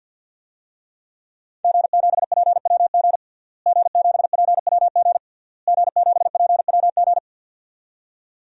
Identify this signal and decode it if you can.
Morse code, heard as a keyed tone, about 48 wpm, 700 Hz. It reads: G7PWG G7PWG G7PWG
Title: DXCC-100 Call Signs - 5 Characters - Repeated 3x - 50wpm